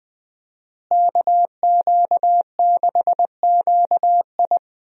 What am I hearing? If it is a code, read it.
Morse code, heard as a keyed tone, about 21 wpm, 700 Hz. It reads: KQ6QI
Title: USA Call Signs - 5 Characters - Repeated 2x - 20wpm